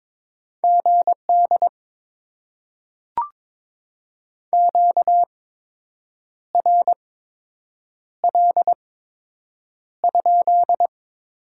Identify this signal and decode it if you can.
Morse code, heard as a keyed tone, about 22 wpm, spaced as for 7 wpm, 700 Hz. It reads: GD QRL?
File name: Top 90 QSO Elements - 8x Character Spacing - 22wpm